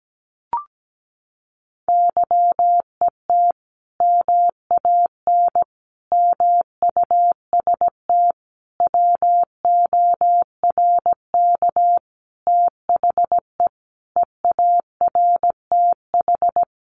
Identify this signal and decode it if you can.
Morse code, heard as a keyed tone, about 17 wpm, 700 Hz. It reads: YET MAN MUST WORK THE EARTH
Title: Sentences from Top 1500 Words 17wpm